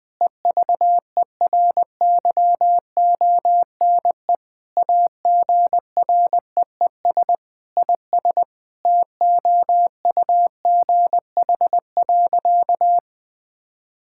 Morse code, heard as a keyed tone, about 20 wpm, 700 Hz. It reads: EVERYONE AGREES IS TOUGH.
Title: Sentences with 7-Word Phrases - 20wpm